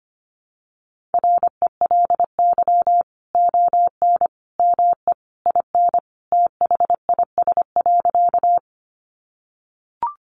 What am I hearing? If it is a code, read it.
Morse code, heard as a keyed tone, about 25 wpm, 700 Hz. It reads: RELY ON ME IN THIS.
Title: Sentences from Top 1200 Words 25wpm